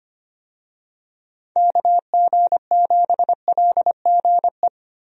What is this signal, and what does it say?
Morse code, heard as a keyed tone, about 25 wpm, 700 Hz. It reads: KG7LGE
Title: USA Call Signs - 6 Characters - Repeated 3x - 25wpm